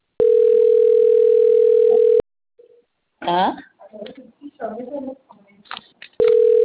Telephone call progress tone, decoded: ringback tone

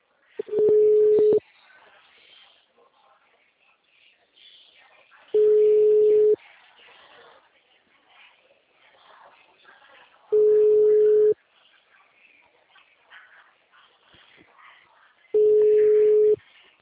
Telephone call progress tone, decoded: ringback tone

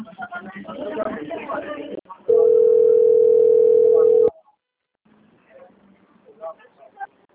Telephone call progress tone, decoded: ringback tone

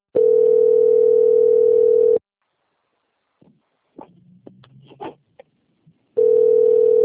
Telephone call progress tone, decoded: ringback tone

